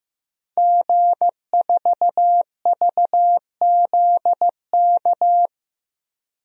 Morse code, heard as a keyed tone, about 15 wpm, 700 Hz. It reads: G4VZK
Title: DXCC-100 Call Signs - 5 Characters - Repeated 2x - 15wpm